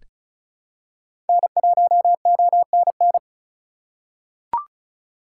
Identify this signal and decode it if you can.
Morse code, heard as a keyed tone, about 34 wpm, 700 Hz. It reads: N1ONN
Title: USA Call Signs - 5 Characters - 2x Character Spacing - 35wpm